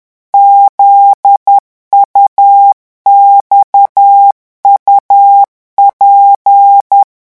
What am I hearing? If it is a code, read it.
Morse code, heard as a keyed tone, about 10 wpm, 800 Hz. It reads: ZUXUP
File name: morse mp3 / revision8-vitesse 10 mots